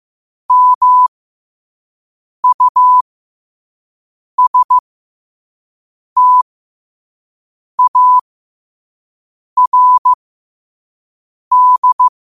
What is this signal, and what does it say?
Morse code, heard as a keyed tone, about 15 wpm, spaced as for 5 wpm, 1000 Hz. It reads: MUSTARD
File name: qotdfiles / 5wpm